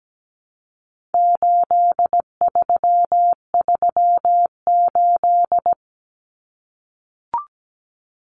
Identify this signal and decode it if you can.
Morse code, heard as a keyed tone, about 17 wpm, 700 Hz. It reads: 8338
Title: Numbers - 4-Digit 17wpm